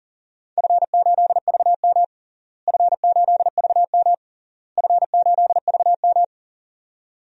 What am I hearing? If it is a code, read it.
Morse code, heard as a keyed tone, about 40 wpm, 700 Hz. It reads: F8VM F8VM F8VM
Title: DXCC-100 Call Signs - 4 Characters - Repeated 3x - 40wpm